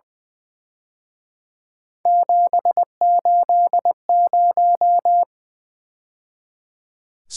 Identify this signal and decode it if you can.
Morse code, heard as a keyed tone, about 25 wpm, 700 Hz. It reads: 780
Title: Speed-Racing - Three Letter-Number Combinations - 20wpm